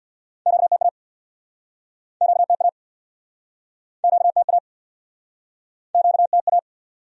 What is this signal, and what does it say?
Morse code, heard as a keyed tone, about 64 wpm, 700 Hz. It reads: QTA QTA QTA QTA